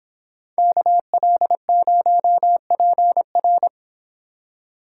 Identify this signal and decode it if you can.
Morse code, heard as a keyed tone, about 26 wpm, 700 Hz. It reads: KL0PR